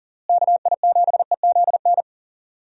Morse code, heard as a keyed tone, about 40 wpm, 700 Hz. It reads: KI7EZN
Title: USA Call Signs - 6 Characters - Repeated 2x - 40wpm